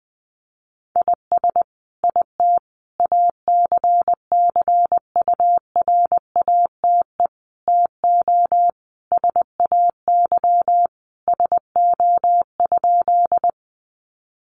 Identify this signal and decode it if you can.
Morse code, heard as a keyed tone, about 20 wpm, 700 Hz. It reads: IS IT ACCURATE TO SAY SO?
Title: Sentences from Top 1200 Words 20wpm